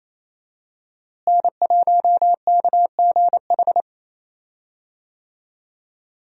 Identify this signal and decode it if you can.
Morse code, heard as a keyed tone, about 28 wpm, 700 Hz. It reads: N1KGH
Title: USA Call Signs - 5 Characters - Repeated 2x - 28wpm